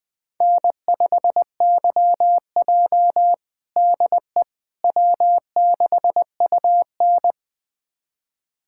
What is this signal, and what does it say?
Morse code, heard as a keyed tone, about 20 wpm, 700 Hz. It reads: N5YJ DE W6UN
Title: USA Callsign DE Callsign - 4 Characters - 20wpm